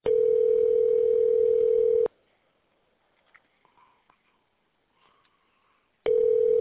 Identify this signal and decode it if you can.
Ringback tone (North America), 440 + 480 Hz.